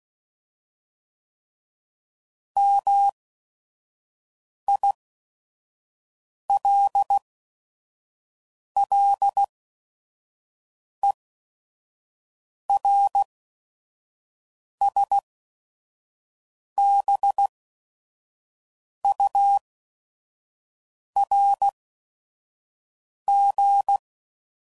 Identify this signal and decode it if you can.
Morse code, heard as a keyed tone, about 16 wpm, spaced as for 5 wpm, 800 Hz. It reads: MILLERSBURG
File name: CW Code Groups @5WPM / fromSM's / Cities